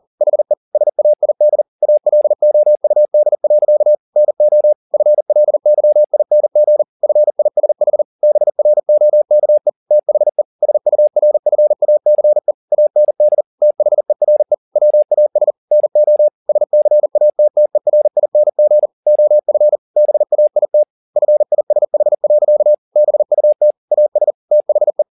Morse code, heard as a keyed tone, about 40 wpm, 600 Hz. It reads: HE SAID ALOUD. NO FLYING FISH BROKE THE SURFACE AND THERE WAS NO SCATTERING OF BAIT FISH. BUT AS THE